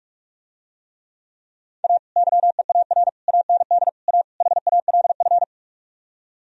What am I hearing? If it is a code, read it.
Morse code, heard as a keyed tone, about 46 wpm, 700 Hz. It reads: A YEAR AND A HALF